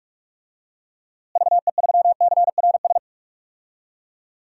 Morse code, heard as a keyed tone, about 46 wpm, 700 Hz. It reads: VE3CRS
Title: DXCC-100 Call Signs - 6 Characters - Repeated 2x - 45wpm